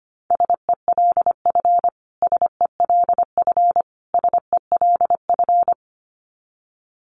Morse code, heard as a keyed tone, about 25 wpm, 700 Hz. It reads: SELF SELF SELF
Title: Top 500 Words Repeated 25wpm